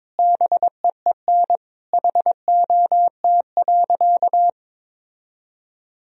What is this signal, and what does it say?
Morse code, heard as a keyed tone, about 22 wpm, 700 Hz. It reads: BEEN HOT.
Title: Sentences from Top 1000 Words - Review Part 3 - 22wpm